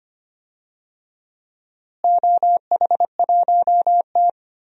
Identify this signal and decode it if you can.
Morse code, heard as a keyed tone, about 25 wpm, 700 Hz. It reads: OH1T
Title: DXCC-100 Call Signs - 4 Characters - Repeated 3x - 25wpm